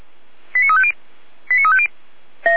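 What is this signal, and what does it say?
Five-tone selective call (ZVEI), heard as quick runs of five tones, tones of about 80 ms. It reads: 89350 89350
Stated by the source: ZVEI five-tone selective call for a fire brigade, received on an SDR.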